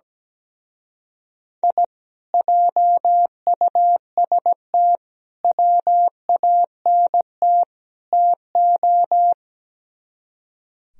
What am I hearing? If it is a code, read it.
Morse code, heard as a keyed tone, about 18 wpm, 700 Hz. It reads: I JUST WANT TO
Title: Speed-Racing - Sets of 4 Words 17wpm